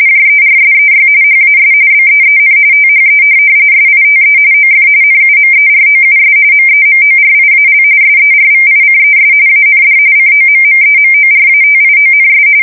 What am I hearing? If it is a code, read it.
RTTY, about 45.45 baud. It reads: JERSEY HAS BEEN THE SITE OF NUMEROUS MEMORIAL SERVICES AND COMMEMORATIONS
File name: fsk170 / USS-New-Jersey